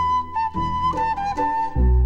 There is no code.